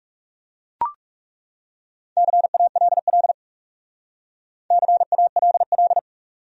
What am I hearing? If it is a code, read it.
Morse code, heard as a keyed tone, about 39 wpm, 700 Hz. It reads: CALL CALL